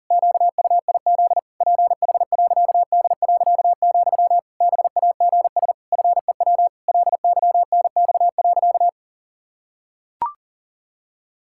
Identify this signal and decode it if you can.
Morse code, heard as a keyed tone, about 40 wpm, 700 Hz. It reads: QUIZ PH.D., BAGS FEW LYNX.
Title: Pangram Challenge v2 - 40wpm